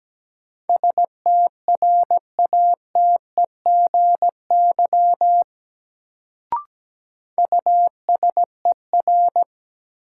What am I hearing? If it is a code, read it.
Morse code, heard as a keyed tone, about 17 wpm, 700 Hz. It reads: STRATEGY USER
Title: Sentences from Top 1000 Words - Review Part 5 - New Word Drill - 17wpm